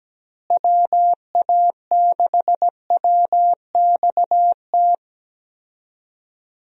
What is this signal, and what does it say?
Morse code, heard as a keyed tone, about 17 wpm, 700 Hz. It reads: WA6WXT
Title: USA Call Signs - 6 Characters - Repeated 2x - 17wpm